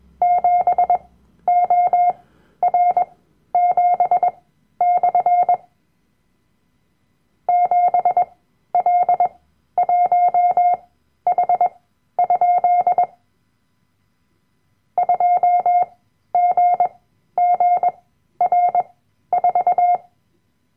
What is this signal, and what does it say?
Morse code, heard as a keyed tone, about 21 wpm, spaced as for 12 wpm, 710 Hz. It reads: 7OR7/ 7L1H? 2GGR4
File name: Workshop Dateien / CW-Online-Kurs / Lektion 010